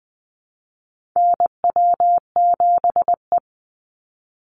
Morse code, heard as a keyed tone, about 20 wpm, 700 Hz. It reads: NW7E
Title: NA SOTA Chaser Callsigns 20wpm